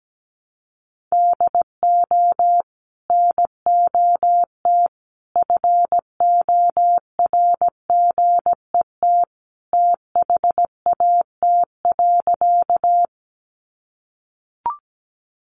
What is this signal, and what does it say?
Morse code, heard as a keyed tone, about 17 wpm, 700 Hz. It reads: DO NOT FORGET THAT.
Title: Sentences from Top 1200 Words 17wpm